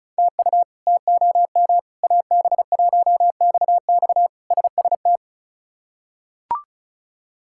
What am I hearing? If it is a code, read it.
Morse code, heard as a keyed tone, about 36 wpm, 700 Hz. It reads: TU TOM AB1XX SST
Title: K1USN SST QSO Practice - 35wpm